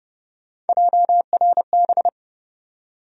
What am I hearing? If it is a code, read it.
Morse code, heard as a keyed tone, about 30 wpm, 700 Hz. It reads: JRB
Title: USA Call Sign Postfixes - 3 Letters 30wpm